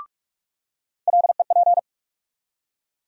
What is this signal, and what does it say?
Morse code, heard as a keyed tone, about 46 wpm, 700 Hz. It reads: LEP
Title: USA Call Sign Postfixes - 3 Letters 45wpm